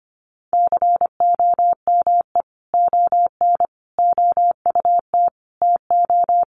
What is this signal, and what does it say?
Morse code, heard as a keyed tone, about 25 wpm, 700 Hz. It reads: COME ON OUT TO
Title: Sentences from Top 1500 Words 25wpm